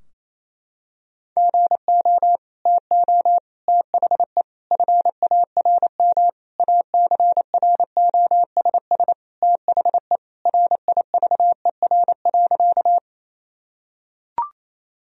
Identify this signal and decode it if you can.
Morse code, heard as a keyed tone, about 28 wpm, 700 Hz. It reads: GO TO THE FARM ACROSS THE RIVER.
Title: Sentences from Top 1000 Words - Review Part 2 - 28wpm